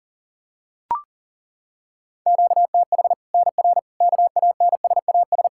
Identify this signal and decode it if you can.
Morse code, heard as a keyed tone, about 40 wpm, 700 Hz. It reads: QTH NR KANSAS
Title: Practice QSO - Exchange 1 - Beginning - 40wpm